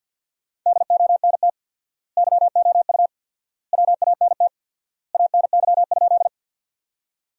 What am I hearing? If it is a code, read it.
Morse code, heard as a keyed tone, about 50 wpm, 700 Hz. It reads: DONT YOU WANT ANY?
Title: 4x Word-Spacing - Sentences from Top 300 Words - 50wpm